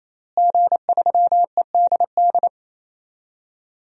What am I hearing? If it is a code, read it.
Morse code, heard as a keyed tone, about 28 wpm, 700 Hz. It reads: G3EDD
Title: DXCC-100 Call Signs - 5 Characters - Repeated 3x - 28wpm